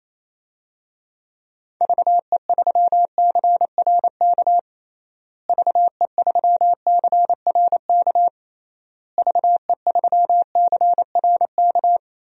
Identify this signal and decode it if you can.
Morse code, heard as a keyed tone, about 27 wpm, 700 Hz. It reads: VE3CRK VE3CRK VE3CRK
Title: DXCC-100 Call Signs - 6 Characters - Repeated 3x - 28wpm